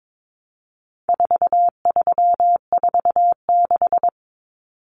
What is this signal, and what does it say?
Morse code, heard as a keyed tone, about 22 wpm, 700 Hz. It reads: 4346
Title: Numbers - 4-Digit 22wpm